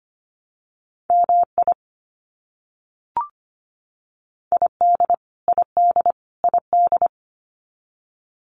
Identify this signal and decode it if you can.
Morse code, heard as a keyed tone, about 25 wpm, 700 Hz. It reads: MI ID ID ID
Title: US States Repeated 3x 25wpm